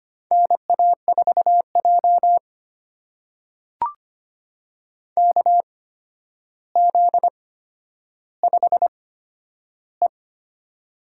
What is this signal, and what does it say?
Morse code, heard as a keyed tone, about 25 wpm, spaced as for 11 wpm, 700 Hz. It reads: NA4J KZ5E
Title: USA Call Signs - 4 Characters - 8x Character Spacing - 25wpm